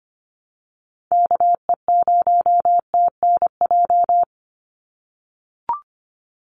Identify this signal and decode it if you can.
Morse code, heard as a keyed tone, about 25 wpm, 700 Hz. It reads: KE0TNJ
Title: USA Call Signs 25wpm